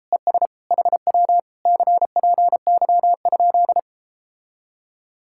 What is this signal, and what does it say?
Morse code, heard as a keyed tone, about 37 wpm, 700 Hz. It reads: ES HW CPY?